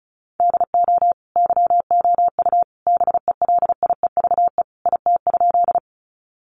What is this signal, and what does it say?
Morse code, heard as a keyed tone, about 34 wpm, 700 Hz. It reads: DO YOU BELIEVE IT?
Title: Sentences from Top 1000 Words 35wpm